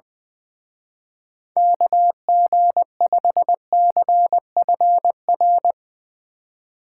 Morse code, heard as a keyed tone, about 21 wpm, 700 Hz. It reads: KG5CFR
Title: Speed Racing - USA 2x3 Call Signs 20wpm